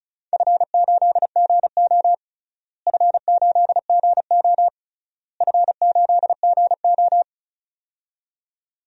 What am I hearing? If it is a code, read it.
Morse code, heard as a keyed tone, about 36 wpm, 700 Hz. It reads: F8GO F8GO F8GO
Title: DXCC-100 Call Signs - 4 Characters - Repeated 3x - 35wpm